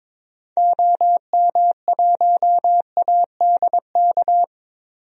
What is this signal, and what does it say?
Morse code, heard as a keyed tone, about 22 wpm, 700 Hz. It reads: OM1ADK
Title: DXCC-100 Call Signs - 6 Characters - 22wpm